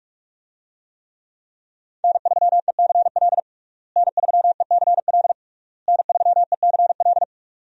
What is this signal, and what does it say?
Morse code, heard as a keyed tone, about 43 wpm, 700 Hz. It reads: N3ECL N3ECL N3ECL
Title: USA Call Signs - 5 Characters - Repeated 3x - 45wpm